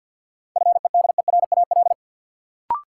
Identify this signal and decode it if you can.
Morse code, heard as a keyed tone, about 52 wpm, 700 Hz. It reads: FEDERAL
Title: Sets of 3 Words - New Word Drill - 50wpm